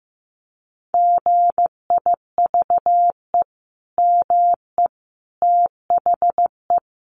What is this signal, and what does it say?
Morse code, heard as a keyed tone, about 15 wpm, 700 Hz. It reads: GIVE ME THE
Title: Sentences from Top 1100 Words 15wpm